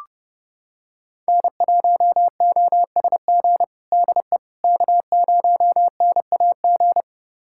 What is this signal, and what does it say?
Morse code, heard as a keyed tone, about 30 wpm, 700 Hz. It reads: N1OSG DE K0NAG
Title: USA Callsign DE Callsign - 5 Characters - 30wpm